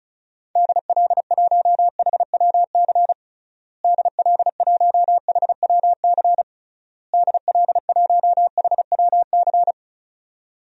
Morse code, heard as a keyed tone, about 34 wpm, 700 Hz. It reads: DL1HWC DL1HWC DL1HWC
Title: DXCC-100 Call Signs - 6 Characters - Repeated 3x - 35wpm